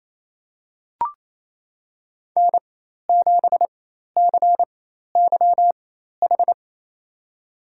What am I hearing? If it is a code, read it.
Morse code, heard as a keyed tone, about 28 wpm, spaced as for 13 wpm, 700 Hz. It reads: N7CYH
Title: USA Call Signs - 4x Character Spacing - 28wpm